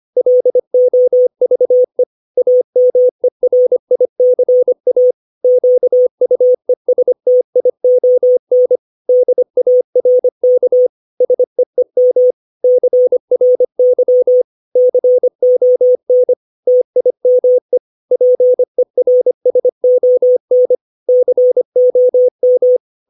Morse code, heard as a keyed tone, about 25 wpm, 500 Hz. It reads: LOVE AMERICA QUESTION DARK SEEM CRY CON TIME PERSON COM
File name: 632 words / 632 Words 25WPM